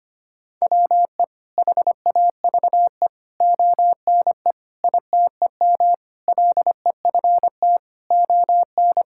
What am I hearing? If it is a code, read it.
Morse code, heard as a keyed tone, about 25 wpm, 700 Hz. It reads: WE HAVE ONE ITEM LEFT ON
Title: Sentences from Top 1000 Words - Review Part 6 - 25wpm